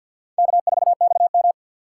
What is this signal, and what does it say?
Morse code, heard as a keyed tone, about 52 wpm, 690 Hz. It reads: K4XM
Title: USA Call Signs - 4x Character Spacing - 50wpm